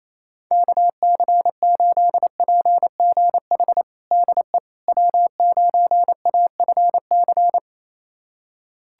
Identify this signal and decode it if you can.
Morse code, heard as a keyed tone, about 28 wpm, 700 Hz. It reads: KC8PGH DE W9AFC
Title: USA Callsign DE Callsign 28wpm